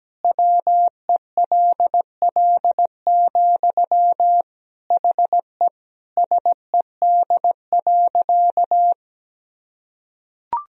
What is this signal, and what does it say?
Morse code, heard as a keyed tone, about 17 wpm, 700 Hz. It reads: WELL, HE SED.
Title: Sentences from Top 500 Words and Abbreviations 17wpm